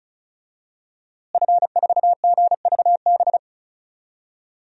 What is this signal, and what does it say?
Morse code, heard as a keyed tone, about 34 wpm, 700 Hz. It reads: F4GVB
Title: DXCC-100 Call Signs - 5 Characters - Repeated 3x - 35wpm